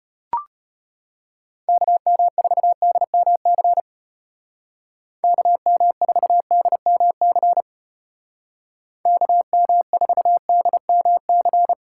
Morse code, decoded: KM4DMC KM4DMC KM4DMC

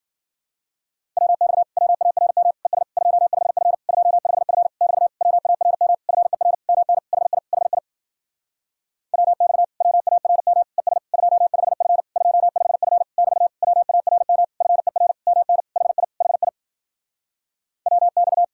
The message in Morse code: WX WARM ES 25F 25F <BT> WARM FER MN HI HI WX WARM ES 25F 25F <BT> WARM FER MN HI HI WX

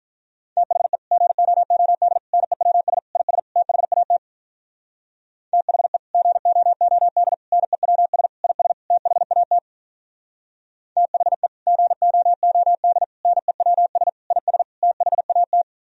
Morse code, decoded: THE GOOD NEWS IS THAT THE GOOD NEWS IS THAT THE GOOD NEWS IS THAT

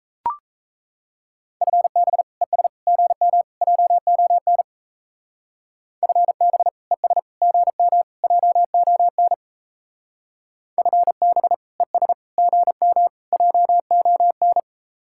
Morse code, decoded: FB ES GM JON FB ES GM JON FB ES GM JON

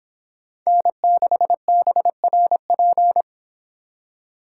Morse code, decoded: N6BRP

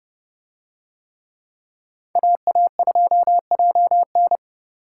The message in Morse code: AA2JN

